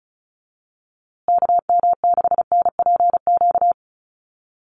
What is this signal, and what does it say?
Morse code, heard as a keyed tone, about 34 wpm, 700 Hz. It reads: KM6NPQ
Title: USA Call Signs Repeated 3x 35wpm